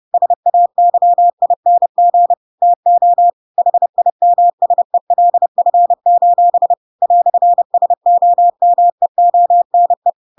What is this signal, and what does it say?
Morse code, heard as a keyed tone, about 30 wpm, 700 Hz. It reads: SAYING TO HIMSELF: "SOMEONE
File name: Pinocchio / Pinocchio ch6 30 30 ik1hge